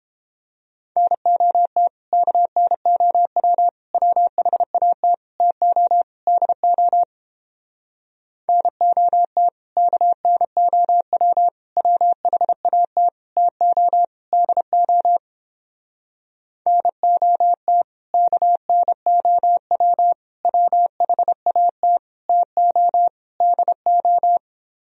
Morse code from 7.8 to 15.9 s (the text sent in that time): NOT KNOW WHAT TO DO